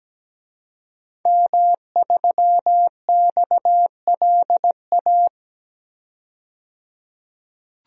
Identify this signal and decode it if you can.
Morse code, heard as a keyed tone, about 17 wpm, 700 Hz. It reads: M3XLA
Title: DXCC-100 Call Signs - 5 Characters - 17wpm